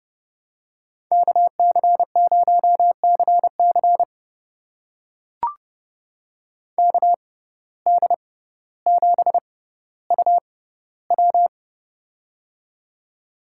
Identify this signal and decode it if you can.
Morse code, heard as a keyed tone, about 30 wpm, spaced as for 15 wpm, 700 Hz. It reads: KC0CC KD7UW
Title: USA Call Signs - 5 Characters - 6x Character Spacing - 30wpm